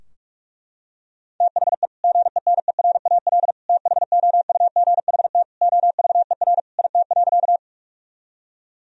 Morse code, heard as a keyed tone, about 42 wpm, 700 Hz. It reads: THE GENERAL THOUGHT OVER IT.
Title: Sentences from Top 1000 Words - Review Part 3 - 45wpm